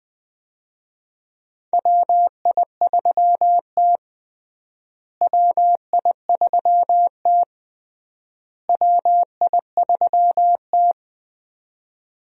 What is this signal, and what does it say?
Morse code, heard as a keyed tone, about 20 wpm, 700 Hz. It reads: WI3T WI3T WI3T
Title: USA Call Signs - 4 Characters - Repeated 3x - 20wpm